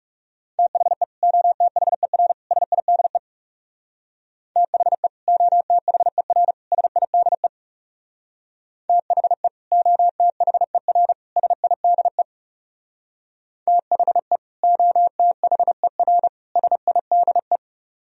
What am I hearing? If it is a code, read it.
Morse code, heard as a keyed tone, about 37 wpm, 700 Hz. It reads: THE OTHER SIDE THE OTHER SIDE THE OTHER SIDE THE OTHER SIDE
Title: Speed-Racing - Sets of 3 Words 30wpm